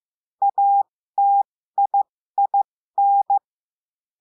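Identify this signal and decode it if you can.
Morse code, heard as a keyed tone, about 15 wpm, spaced as for 13 wpm, 800 Hz. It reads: ATIIN
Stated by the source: Morse code lesson adding the letter I, with five-letter code groups.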